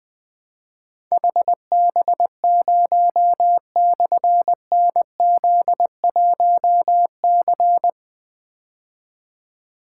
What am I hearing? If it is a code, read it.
Morse code, heard as a keyed tone, about 20 wpm, 700 Hz. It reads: HB0/NZ1C
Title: Difficult Call Signs 20wpm